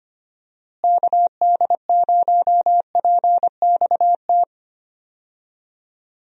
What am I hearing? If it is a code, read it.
Morse code, heard as a keyed tone, about 25 wpm, 700 Hz. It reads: KD0PXT
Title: USA Call Signs - 6 Characters - Repeated 2x - 25wpm